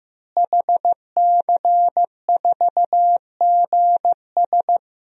Morse code, heard as a keyed tone, about 15 wpm, 700 Hz. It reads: HC4GS
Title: DXCC-100 Call Signs - 5 Characters - Repeated 3x - 15wpm